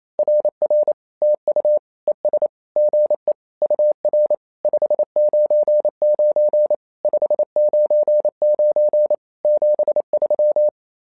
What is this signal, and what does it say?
Morse code, heard as a keyed tone, about 27 wpm, 600 Hz. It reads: RR TU ES GE UR 599 599 73
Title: SOTA QSO Practice 28wpm